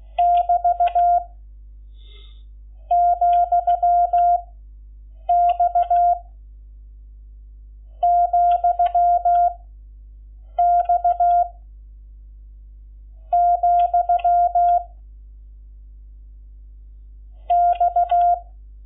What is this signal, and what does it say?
Morse code, heard as a keyed tone, about 16 wpm, 690 Hz. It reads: <BT> , X , X , X
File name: Lecciones / CLASE Nº11